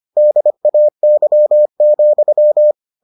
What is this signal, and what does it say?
Morse code, heard as a keyed tone, about 25 wpm, 600 Hz. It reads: DAY,